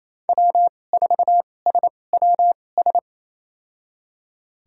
Morse code, heard as a keyed tone, about 28 wpm, spaced as for 20 wpm, 700 Hz. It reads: W4SWS
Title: USA Call Signs - 5 Characters - 2x Character Spacing - 28wpm